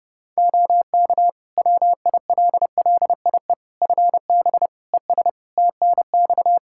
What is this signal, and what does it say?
Morse code, heard as a keyed tone, about 32 wpm, 700 Hz. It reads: OK WILLIE FB ES TNX
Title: Practice QSO - Exchange 2 - Info - 30wpm